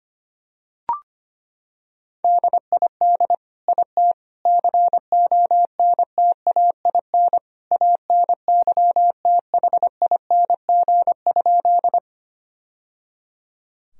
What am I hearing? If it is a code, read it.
Morse code, heard as a keyed tone, about 25 wpm, 700 Hz. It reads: DID IT CONTAIN ANYTHING?
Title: Sentences from Top 1000 Words - Review Part 2 - 25wpm